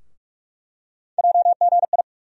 Morse code, heard as a keyed tone, about 46 wpm, 700 Hz. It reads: JGI
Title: USA Call Sign Postfixes - 3 Letters 45wpm